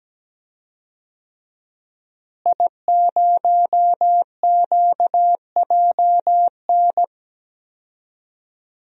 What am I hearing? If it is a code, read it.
Morse code, heard as a keyed tone, about 17 wpm, 700 Hz. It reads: I0QJN
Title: DXCC-100 Call Signs - 5 Characters - Repeated 3x - 17wpm